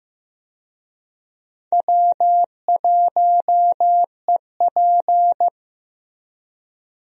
Morse code, heard as a keyed tone, about 15 wpm, 700 Hz. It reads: W1EP
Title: USA Call Signs - 4 Characters - Repeated 3x - 15wpm